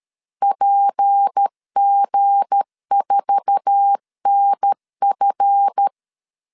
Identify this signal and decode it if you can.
Morse code, heard as a keyed tone, about 13 wpm, 800 Hz. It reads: PG4NF
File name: CW / DET / 1 9 1500C S B 12wpm 3819